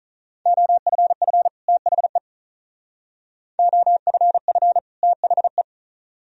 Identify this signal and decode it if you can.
Morse code, heard as a keyed tone, about 44 wpm, 700 Hz. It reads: OFF THE OFF THE